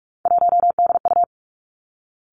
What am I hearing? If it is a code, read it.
Morse code, heard as a keyed tone, about 46 wpm, 700 Hz. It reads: 1DU
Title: Three Challenging Letter-Number Combinations 45wpm